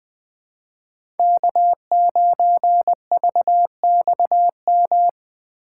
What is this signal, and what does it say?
Morse code, heard as a keyed tone, about 20 wpm, 700 Hz. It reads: K9VXM
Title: USA Call Signs - 5 Characters - 4x Character Spacing - 20wpm